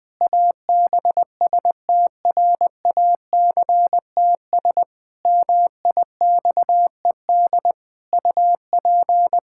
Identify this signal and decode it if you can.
Morse code, heard as a keyed tone, about 20 wpm, 700 Hz. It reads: ABSTRACTS MIXED UP
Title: Pangram Challenge v2 - 20wpm